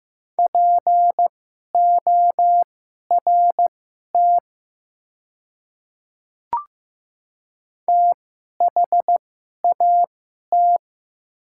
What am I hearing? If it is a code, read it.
Morse code, heard as a keyed tone, about 15 wpm, spaced as for 11 wpm, 700 Hz. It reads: PORT THAT
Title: 2x Character Spacing - Top 200 Words - 15wpm